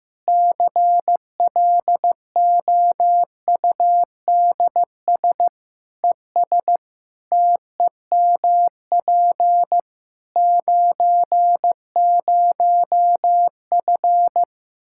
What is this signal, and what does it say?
Morse code, heard as a keyed tone, about 16 wpm, 700 Hz. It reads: CLOUDS ES TEMP 90F
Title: Practice QSO - Exchange 2 - Info - 15wpm